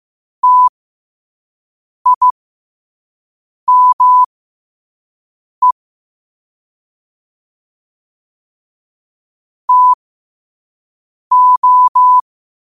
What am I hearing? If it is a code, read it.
Morse code, heard as a keyed tone, about 15 wpm, spaced as for 5 wpm, 1000 Hz. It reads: TIME TO
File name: qotdfiles / 5wpm